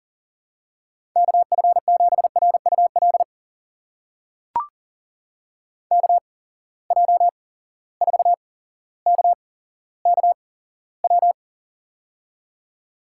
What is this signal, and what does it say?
Morse code, heard as a keyed tone, about 43 wpm, spaced as for 17 wpm, 700 Hz. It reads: KF7RUL KJ4KKW
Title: USA Call Signs - 6 Characters - 8x Character Spacing - 40wpm